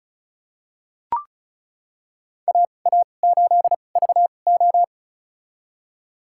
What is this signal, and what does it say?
Morse code, heard as a keyed tone, about 34 wpm, spaced as for 25 wpm, 700 Hz. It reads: AA8VO
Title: USA Call Signs - 5 Characters - 2x Character Spacing - 35wpm